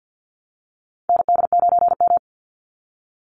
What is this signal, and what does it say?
Morse code, heard as a keyed tone, about 50 wpm, 700 Hz. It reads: ND9M
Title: Call Signs Repeated 2x - 50wpm